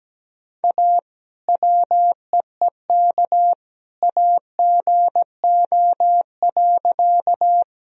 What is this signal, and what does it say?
Morse code, heard as a keyed tone, about 17 wpm, 700 Hz. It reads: A WEEK AGO.
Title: Sentences from Top 1000 Words - Review Part 3 - 17wpm